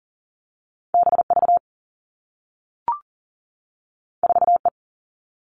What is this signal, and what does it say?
Morse code, heard as a keyed tone, about 42 wpm, 700 Hz. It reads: B V 4 E
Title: Two Challenging Letter-Number Combinations 40wpm